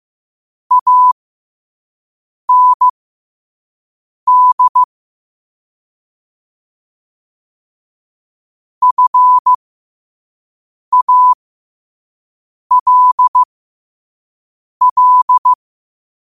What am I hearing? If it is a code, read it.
Morse code, heard as a keyed tone, about 15 wpm, spaced as for 5 wpm, 1000 Hz. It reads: AND FALL
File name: qotdfiles / 5wpm